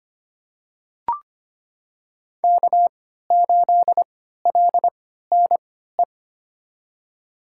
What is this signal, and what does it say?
Morse code, heard as a keyed tone, about 25 wpm, spaced as for 14 wpm, 700 Hz. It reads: K8LNE
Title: USA Call Signs - 3x Character Spacing - 25wpm